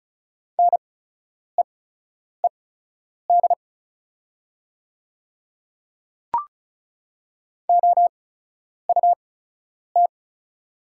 Morse code, heard as a keyed tone, about 34 wpm, spaced as for 10 wpm, 690 Hz. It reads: NEED OUT
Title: 8x Character Spacing - Top 200 Words - 35wpm